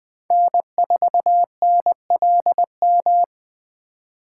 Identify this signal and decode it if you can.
Morse code, heard as a keyed tone, about 20 wpm, 700 Hz. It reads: N4NLM